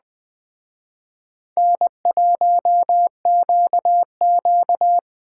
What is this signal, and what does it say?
Morse code, heard as a keyed tone, about 20 wpm, 700 Hz. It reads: N1QQ
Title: Speed Racing - USA 1x2 Call Signs 17wpm